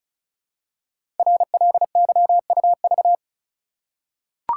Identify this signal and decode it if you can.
Morse code, heard as a keyed tone, about 36 wpm, 700 Hz. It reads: RLYUV
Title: Code Groups - 5 Letters - 35wpm